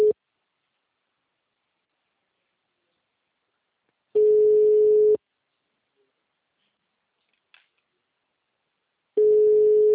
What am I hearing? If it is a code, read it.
Ringback tone (Europe, 425 Hz).